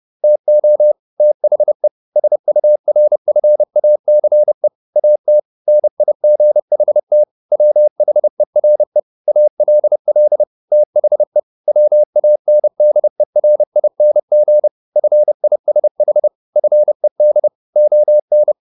Morse code, heard as a keyed tone, about 30 wpm, 600 Hz. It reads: TO THE SURFACE AT NIGHT WHERE ALL THE WANDERING FISH FED ON
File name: the old man and the sea / the old man and the sea 1 30wpm